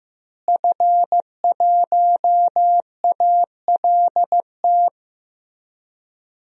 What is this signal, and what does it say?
Morse code, heard as a keyed tone, about 15 wpm, 700 Hz. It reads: F1ALT
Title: DXCC-100 Call Signs - 5 Characters - Repeated 3x - 15wpm